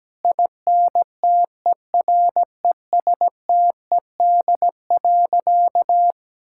Morse code, heard as a keyed tone, about 17 wpm, 700 Hz. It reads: INTERESTED.
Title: Sentences from Top 1000 Words - Review Part 6 - 17wpm